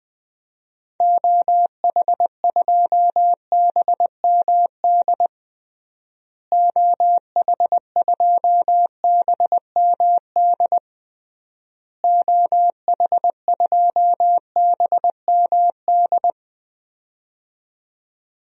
Morse code, heard as a keyed tone, about 20 wpm, 700 Hz. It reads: OH2BMD OH2BMD OH2BMD
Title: DXCC-100 Call Signs - 6 Characters - Repeated 3x - 20wpm